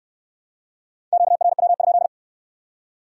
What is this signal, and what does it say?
Morse code, heard as a keyed tone, about 63 wpm, 700 Hz. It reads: QRW?